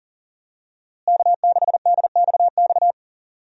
Morse code, heard as a keyed tone, about 40 wpm, 700 Hz. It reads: K6DXX
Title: USA Call Signs - 5 Characters - Repeated 3x - 40wpm